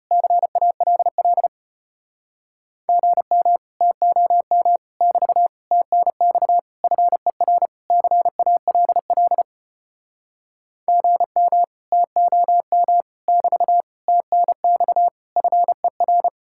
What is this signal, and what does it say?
Morse code, heard as a keyed tone, about 33 wpm, 700 Hz. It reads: CALL GM TOM <BT> TNX FER CALL GM TOM <BT> TNX FER